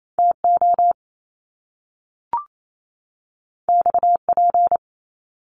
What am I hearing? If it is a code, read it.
Morse code, heard as a keyed tone, about 28 wpm, 700 Hz. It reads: TO XP
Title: Top Two Letter Combinations in Words 28wpm